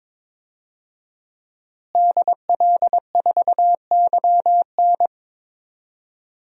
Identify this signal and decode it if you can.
Morse code, heard as a keyed tone, about 22 wpm, 700 Hz. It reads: DL4YN
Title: DXCC-100 Call Signs - 5 Characters - Repeated 2x - 22wpm